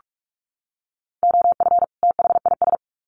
Morse code, heard as a keyed tone, about 44 wpm, 700 Hz. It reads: OF THIS
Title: Sets of 2 Words 45wpm